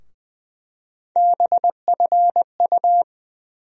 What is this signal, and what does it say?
Morse code, heard as a keyed tone, about 20 wpm, 700 Hz. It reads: BFU
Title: USA Call Sign Postfixes - 3 Letters 20wpm